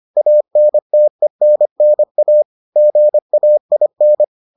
Morse code, heard as a keyed tone, about 25 wpm, 600 Hz. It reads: ANTENNA GAIN